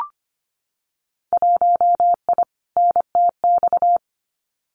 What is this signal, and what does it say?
Morse code, heard as a keyed tone, about 25 wpm, 700 Hz. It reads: 1I NTX
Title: Winter Field Day Exchanges 25wpm